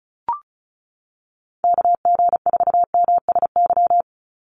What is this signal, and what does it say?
Morse code, heard as a keyed tone, about 34 wpm, 700 Hz. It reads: KG4MSY
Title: USA Call Signs 35wpm